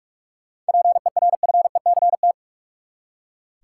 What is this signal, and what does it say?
Morse code, heard as a keyed tone, about 42 wpm, 700 Hz. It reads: PERFECT
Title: Seven Letter Words - 45wpm